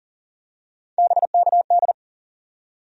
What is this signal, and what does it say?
Morse code, heard as a keyed tone, about 40 wpm, 700 Hz. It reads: BKD
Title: USA Call Sign Postfixes - 3 Letters 40wpm